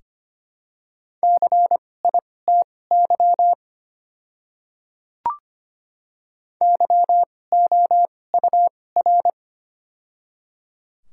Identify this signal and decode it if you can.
Morse code, heard as a keyed tone, about 25 wpm, spaced as for 18 wpm, 700 Hz. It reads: CITY YOUR
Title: 2x Character Spacing - Top 300 Words - 25wpm